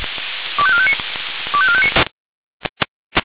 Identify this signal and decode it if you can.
Five-tone selective call (ZVEI), heard as quick runs of five tones, tones of about 70 ms. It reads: 25459 25459